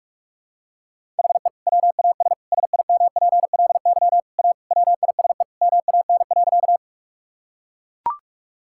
Morse code, heard as a keyed tone, about 41 wpm, 700 Hz. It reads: HE WAS SIMPLY A WISE MAN.
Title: Sentences from Top 1600 Words 45wpm